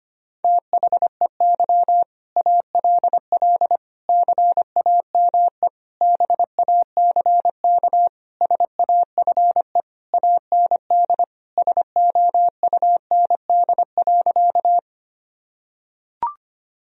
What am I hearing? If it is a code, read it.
Morse code, heard as a keyed tone, about 25 wpm, 700 Hz. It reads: THEY ALL CAME BACK SAFE AND SOUND.
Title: Sentences from Top 1000 Words - Review Part 1 - 25wpm